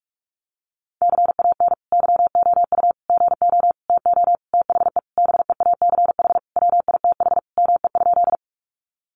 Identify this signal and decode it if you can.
Morse code, heard as a keyed tone, about 44 wpm, 700 Hz. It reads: CAN YOU GO TO THE BEACH WITH ME?
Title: Sentences from Top 1500 Words 45wpm